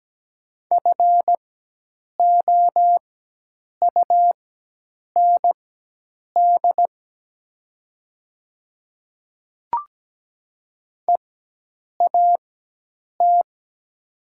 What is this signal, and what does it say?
Morse code, heard as a keyed tone, about 17 wpm, spaced as for 8 wpm, 700 Hz. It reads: FOUND EAT
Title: 4x Character Spacing - Top 300 Words - 17wpm